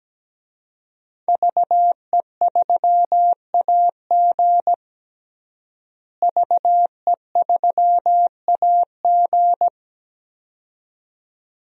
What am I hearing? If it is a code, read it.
Morse code, heard as a keyed tone, about 17 wpm, 700 Hz. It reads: VE3AG VE3AG
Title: DXCC-100 Call Signs - 5 Characters - Repeated 2x - 17wpm